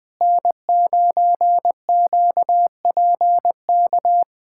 Morse code, heard as a keyed tone, about 20 wpm, 700 Hz. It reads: N9QPK
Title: USA Callsign DE Callsign 20wpm